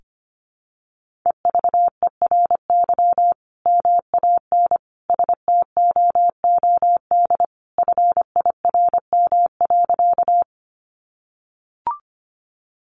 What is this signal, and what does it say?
Morse code, heard as a keyed tone, about 25 wpm, 700 Hz. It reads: EVERY MAN STOOD FIRM.
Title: Sentences from Top 1400 Words 25wpm